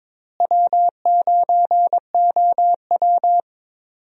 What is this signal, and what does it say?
Morse code, heard as a keyed tone, about 22 wpm, 700 Hz. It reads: W9OW
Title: USA Call Signs - 4 Characters - Repeated 2x - 22wpm